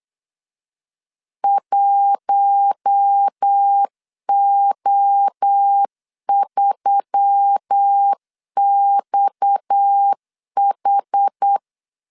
Morse code, heard as a keyed tone, about 8 wpm, 800 Hz. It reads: 1O3XH
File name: LEZIONI / 7 GF9Z / 7 GF9Z 8wpm 3213